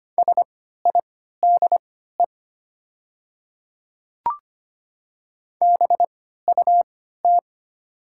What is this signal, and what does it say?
Morse code, heard as a keyed tone, about 25 wpm, spaced as for 14 wpm, 700 Hz. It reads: SIDE BUT
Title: 3x Character Spacing - Top 100 Words - 25wpm